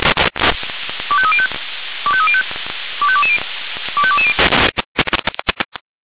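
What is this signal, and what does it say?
Five-tone selective call (ZVEI), heard as quick runs of five tones, tones of about 70 ms. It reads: 25305 25305 25220 25220
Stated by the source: ZVEI five-tone call 25310 received by SDR, channel 496.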